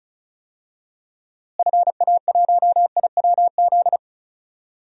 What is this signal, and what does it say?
Morse code, heard as a keyed tone, about 34 wpm, 700 Hz. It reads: FA1IWZ
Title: DXCC-100 Call Signs - 6 Characters - Repeated 3x - 35wpm